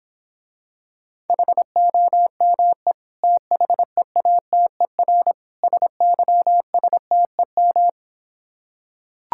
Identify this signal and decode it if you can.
Morse code, heard as a keyed tone, about 26 wpm, 700 Hz. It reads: HOME THEATER SYSTEM